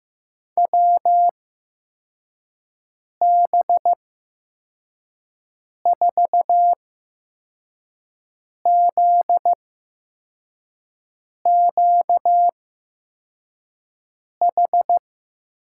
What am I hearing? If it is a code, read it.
Morse code, heard as a keyed tone, about 15 wpm, spaced as for 4 wpm, 700 Hz. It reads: WB4ZQH
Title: USA Call Signs - 6 Characters - 8x Character Spacing - 15wpm